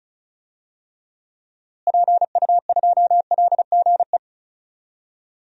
Morse code, heard as a keyed tone, about 34 wpm, 700 Hz. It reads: PU2LGE
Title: DXCC-100 Call Signs - 6 Characters - Repeated 2x - 35wpm